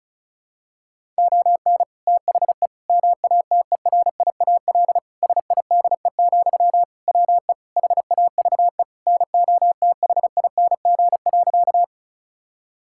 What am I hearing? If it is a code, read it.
Morse code, heard as a keyed tone, about 34 wpm, 700 Hz. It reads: ON THE MATERIAL SIDE, WE HAVE NOTHING.
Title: Sentences from Top 1000 Words - Review Part 5 - 35wpm